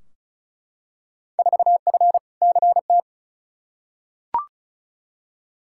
Morse code, heard as a keyed tone, about 36 wpm, 700 Hz. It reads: 4F CT
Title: Field Day Exchanges Encore - 35wpm